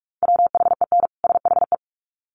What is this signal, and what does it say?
Morse code, heard as a keyed tone, about 44 wpm, 700 Hz. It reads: WHEN SHE
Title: Sets of 2 Words 45wpm